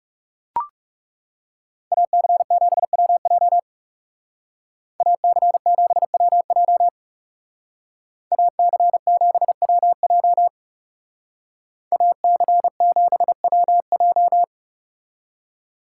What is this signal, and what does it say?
Morse code, heard as a keyed tone, about 36 wpm, 700 Hz. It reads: AC7WJ AC7WJ AC7WJ AC7WJ
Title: Speed Racing - USA 2x2 Call Signs 30wpm